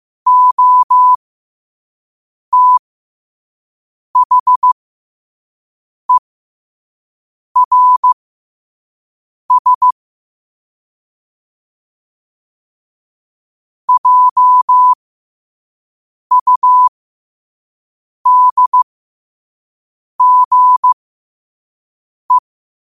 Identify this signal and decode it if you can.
Morse code, heard as a keyed tone, about 15 wpm, spaced as for 5 wpm, 1000 Hz. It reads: OTHERS JUDGE
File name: qotdfiles / 5wpm